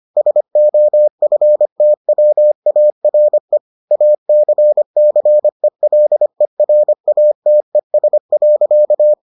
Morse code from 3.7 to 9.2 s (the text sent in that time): ACCELERATES.